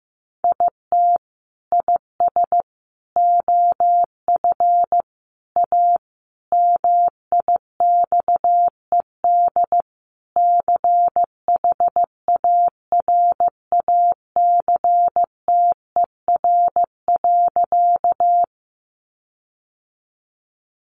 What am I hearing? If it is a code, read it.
Morse code, heard as a keyed tone, about 15 wpm, 700 Hz. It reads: IT IS OF A MIXED CHARACTER.
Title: Sentences from Top 1500 Words 15wpm